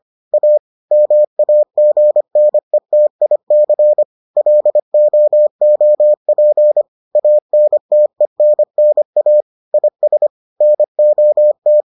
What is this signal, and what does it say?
Morse code, heard as a keyed tone, about 25 wpm, 600 Hz. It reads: A MAGNETIC LOOP ANTENNA IS NOT